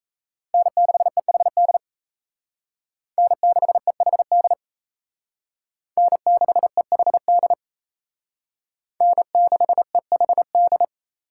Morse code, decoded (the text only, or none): N6EHD N6EHD N6EHD N6EHD